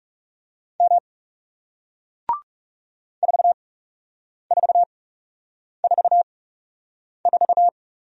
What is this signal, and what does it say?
Morse code, heard as a keyed tone, about 40 wpm, spaced as for 8 wpm, 700 Hz. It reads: M 4444